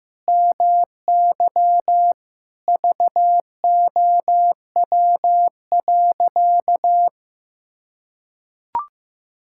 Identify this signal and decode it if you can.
Morse code, heard as a keyed tone, about 16 wpm, 700 Hz. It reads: MY VOW.
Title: Pangram Challenge v2 - 15wpm